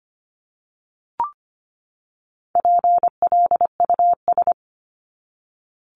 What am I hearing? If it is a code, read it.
Morse code, heard as a keyed tone, about 25 wpm, 700 Hz. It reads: PLUS
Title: Top 1001-1250 Words 25wpm